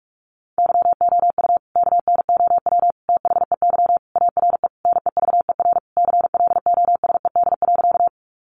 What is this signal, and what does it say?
Morse code, heard as a keyed tone, about 44 wpm, 700 Hz. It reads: YOU KNOW THEY ARE NEVER CLOSED.
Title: Sentences from Top 1400 Words 45wpm